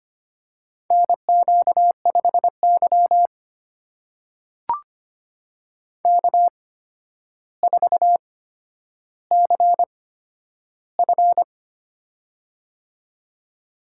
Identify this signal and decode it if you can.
Morse code, heard as a keyed tone, about 25 wpm, spaced as for 11 wpm, 700 Hz. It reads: NQ5Y K4CF
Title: USA Call Signs - 4 Characters - 8x Character Spacing - 25wpm